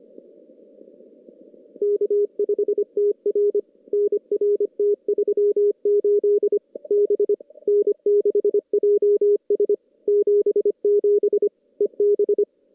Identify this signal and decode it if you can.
Morse code, heard as a keyed tone, about 25 wpm, 1190 Hz, with noise. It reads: K5TR NR T38 B N6JS 77 EB
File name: contests / CW-SS-k5tr- -2300